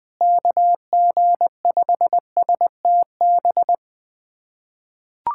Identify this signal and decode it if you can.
Morse code, heard as a keyed tone, about 20 wpm, 700 Hz. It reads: KG5STB
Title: USA Callsign DE Callsign - 6 Characters - 20wpm